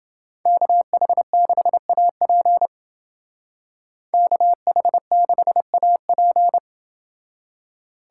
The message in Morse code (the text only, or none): KH6AP KH6AP